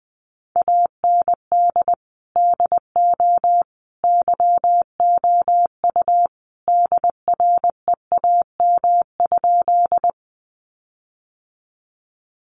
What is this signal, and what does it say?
Morse code, heard as a keyed tone, about 20 wpm, 700 Hz. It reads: AND DO YOU DREAM?
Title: Sentences from Top 1500 Words 20wpm